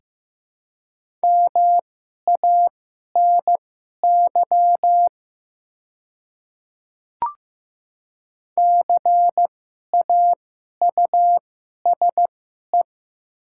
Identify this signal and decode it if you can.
Morse code, heard as a keyed tone, about 15 wpm, spaced as for 11 wpm, 700 Hz. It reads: MANY CAUSE
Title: 2x Character Spacing - Top 200 Words - 15wpm